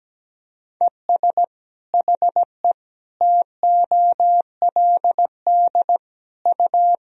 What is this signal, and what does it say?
Morse code, heard as a keyed tone, about 17 wpm, 700 Hz. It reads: ES HE TOLD U
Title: Sentences from Top 500 Words and Abbreviations 17wpm